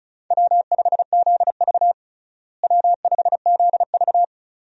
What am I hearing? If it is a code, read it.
Morse code, heard as a keyed tone, about 34 wpm, 700 Hz. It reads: W5ZV W5ZV
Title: USA Call Signs - 4 Characters - Repeated 2x - 35wpm